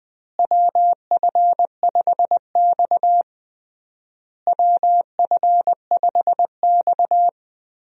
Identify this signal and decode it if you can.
Morse code, heard as a keyed tone, about 20 wpm, 700 Hz. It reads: WF5X WF5X
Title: USA Call Signs - 4 Characters - Repeated 3x - 20wpm